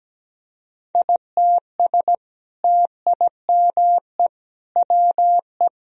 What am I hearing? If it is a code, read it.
Morse code, heard as a keyed tone, about 17 wpm, 700 Hz. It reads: ITS TIME WE
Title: Sentences from Top 1000 Words - Review Part 4 - 17wpm